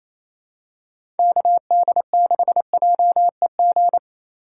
Morse code, heard as a keyed tone, about 28 wpm, 700 Hz. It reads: KD6JEG
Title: USA Call Signs - 6 Characters - 2x Character Spacing - 28wpm